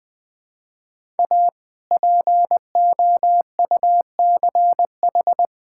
Morse code, heard as a keyed tone, about 21 wpm, 700 Hz. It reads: A POUCH
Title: All-Age Appropriate Jokes - 20wpm